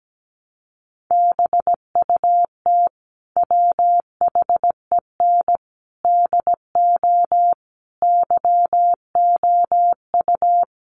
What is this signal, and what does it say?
Morse code, heard as a keyed tone, about 17 wpm, 700 Hz. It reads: BUT WHEN DO YOU
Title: Sentences from Top 1100 Words 17wpm